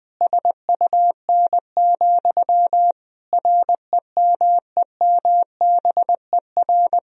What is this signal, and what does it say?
Morse code, heard as a keyed tone, about 20 wpm, 700 Hz. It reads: SUN, REMEMBER
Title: Sentences from Top 1000 Words - Review Part 1 - 20wpm